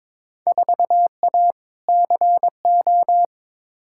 Morse code, heard as a keyed tone, about 22 wpm, 700 Hz. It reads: 4A CO
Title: Field Day Exchanges - 22wpm